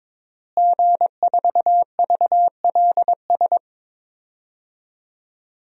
Morse code, heard as a keyed tone, about 22 wpm, 700 Hz. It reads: G4VLS
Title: DXCC-100 Call Signs - 5 Characters - Repeated 2x - 22wpm